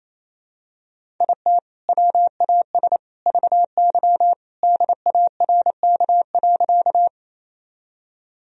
Morse code, heard as a keyed tone, about 28 wpm, 700 Hz. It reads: IT WAS VY DARK.
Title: Sentences from Top 500 Words and Abbreviations 28wpm